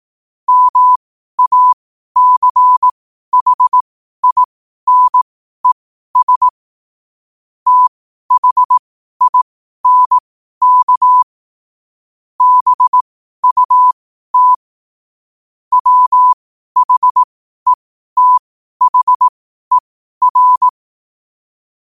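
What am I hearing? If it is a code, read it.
Morse code, heard as a keyed tone, about 18 wpm, spaced as for 12 wpm, 1000 Hz. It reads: MACHINES THINK BUT WHETHER
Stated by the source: Morse code at 12 WPM.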